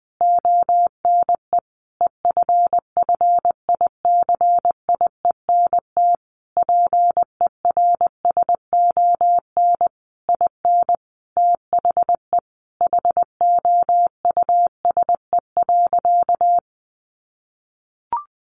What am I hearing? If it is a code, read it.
Morse code, heard as a keyed tone, about 20 wpm, 700 Hz. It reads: ONE EFFICIENT PERSON IN THE HOUSE.
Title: Sentences from Top 1300 Words 20wpm